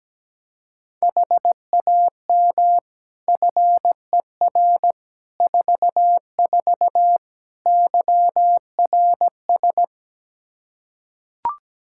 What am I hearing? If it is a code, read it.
Morse code, heard as a keyed tone, about 17 wpm, 700 Hz. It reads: HAM FER 44 YRS
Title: K1USN SST QSO Practice 17wpm